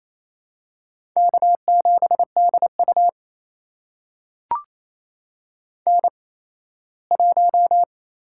Morse code, heard as a keyed tone, about 28 wpm, spaced as for 17 wpm, 700 Hz. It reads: K7DU N1
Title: USA Call Signs - 4 Characters - 8x Character Spacing - 28wpm